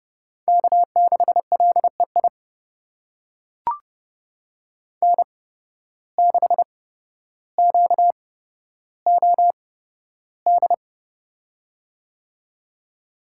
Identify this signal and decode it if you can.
Morse code, heard as a keyed tone, about 30 wpm, spaced as for 13 wpm, 700 Hz. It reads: K6LEI N6QOD
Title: USA Call Signs - 5 Characters - 8x Character Spacing - 30wpm